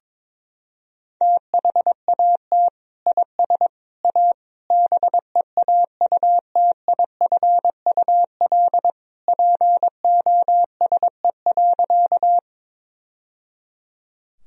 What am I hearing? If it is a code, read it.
Morse code, heard as a keyed tone, about 22 wpm, 700 Hz. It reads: THAT IS A BEAUTIFUL POSE.
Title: Sentences from Top 1000 Words - Review Part 2 - 22wpm